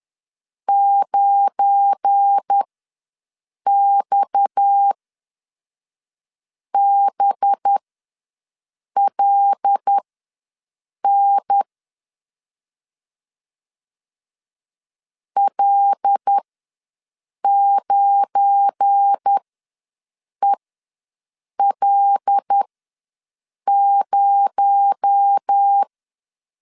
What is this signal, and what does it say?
Morse code, heard as a keyed tone, about 10 wpm, spaced as for 5 wpm, 800 Hz. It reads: 9XBLN L9EL0
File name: LEZIONI / 7 GF9Z / 7 GF9 ws 10wpm 4034